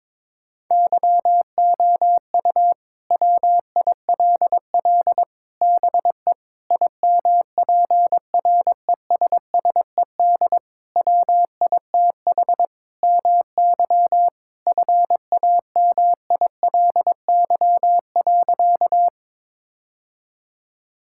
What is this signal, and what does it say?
Morse code, heard as a keyed tone, about 22 wpm, 700 Hz. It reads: YOU WILL BE IMPRESSED WITH MY FAMILY.
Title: Sentences from Top 1600 Words 22wpm